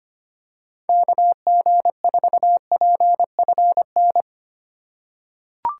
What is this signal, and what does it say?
Morse code, heard as a keyed tone, about 25 wpm, 700 Hz. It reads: KG4PFN
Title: USA Call Signs - 6 Characters - 2x Character Spacing - 25wpm